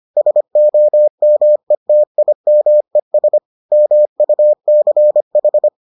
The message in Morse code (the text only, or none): SOMETIMES MUCH